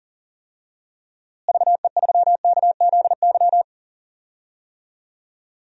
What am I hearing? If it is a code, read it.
Morse code, heard as a keyed tone, about 40 wpm, 700 Hz. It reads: VE3KZY
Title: DXCC-100 Call Signs - 6 Characters - Repeated 2x - 40wpm